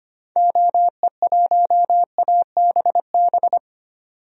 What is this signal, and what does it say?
Morse code, heard as a keyed tone, about 25 wpm, 700 Hz. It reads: OE1ABB
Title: DXCC-100 Call Signs - 6 Characters - 25wpm